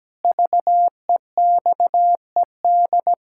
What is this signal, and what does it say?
Morse code, heard as a keyed tone, about 17 wpm, 700 Hz. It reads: VEXED
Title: Pangram Challenge v2 - 17wpm